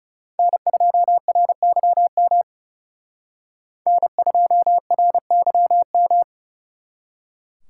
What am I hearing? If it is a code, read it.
Morse code, heard as a keyed tone, about 38 wpm, 700 Hz. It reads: N2RYM N2RYM